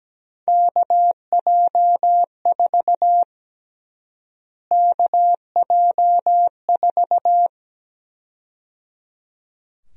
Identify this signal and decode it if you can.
Morse code, heard as a keyed tone, about 17 wpm, 700 Hz. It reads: KJ4 KJ4
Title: USA Call Sign Prefixes - Repeated 2x - 17wpm